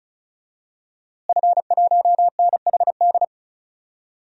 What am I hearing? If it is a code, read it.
Morse code, heard as a keyed tone, about 36 wpm, 700 Hz. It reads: F1NHD
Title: DXCC-100 Call Signs - 5 Characters - Repeated 2x - 35wpm